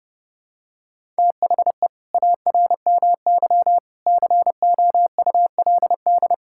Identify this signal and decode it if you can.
Morse code, heard as a keyed tone, about 29 wpm, 700 Hz. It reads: THE ARMY COULD
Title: Sentences from Top 1000 Words - Review Part 5 - 30wpm